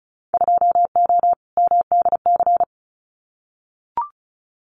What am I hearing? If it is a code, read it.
Morse code, heard as a keyed tone, about 36 wpm, 700 Hz. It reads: 2O MDC
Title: Winter Field Day Exchanges Encore 35wpm